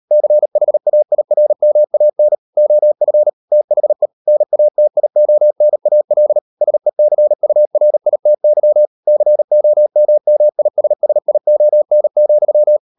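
Morse code, heard as a keyed tone, about 38 wpm, 600 Hz. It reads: CHAIRMAN OF THE NATIONAL SECURITY COMMISSION,
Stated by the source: Morse code practice at 35–39 wpm sending English text of Ken Follett's novel Never, from CW med Gnister.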